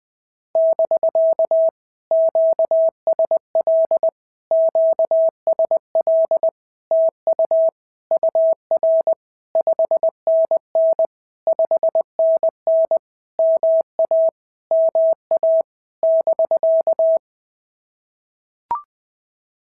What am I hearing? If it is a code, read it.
Morse code, heard as a keyed tone, about 20 wpm, 650 Hz. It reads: <BK> QSL QSL TU UR 5NN 5NN MA MA <BK>
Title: POTA QSO Practice - 20wpm